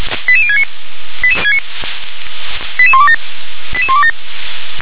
Five-tone selective call (ZVEI), heard as quick runs of five tones, tones of about 70 ms. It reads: 88069 88127 88127